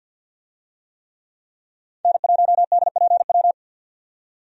Morse code, heard as a keyed tone, about 48 wpm, 700 Hz. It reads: N1DPW
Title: USA Call Signs - 5 Characters - 50wpm